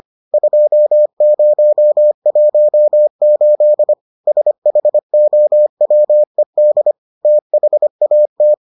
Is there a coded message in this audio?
Yes: Morse code from 0.3 to 8.5 s, heard as a keyed tone, about 25 wpm, 600 Hz. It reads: 2018 SHOWED THAT